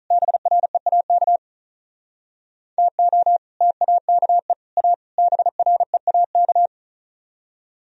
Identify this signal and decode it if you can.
Morse code, heard as a keyed tone, about 45 wpm, 700 Hz. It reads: BREAK TO TAKE A BREAK